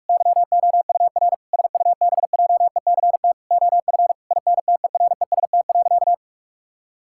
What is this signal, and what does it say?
Morse code, heard as a keyed tone, about 46 wpm, 700 Hz. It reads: YOUR SUBJECT OF INTEREST.
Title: Sentences from Top 1000 Words - Review Part 5 - 45wpm